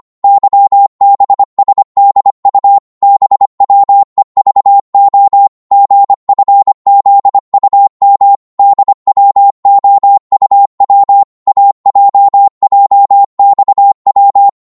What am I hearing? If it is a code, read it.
Morse code, heard as a keyed tone, about 25 wpm, 800 Hz. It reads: YBSDU BWEVO GFZUM DWOUW AJJXW